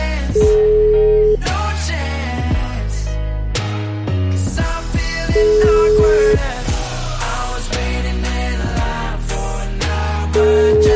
Ringback tone (Europe, 425 Hz), with noise.